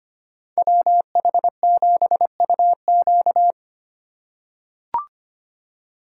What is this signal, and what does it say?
Morse code, heard as a keyed tone, about 25 wpm, 700 Hz. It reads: WH7UQ
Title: USA Call Signs - 5 Characters - 4x Character Spacing - 25wpm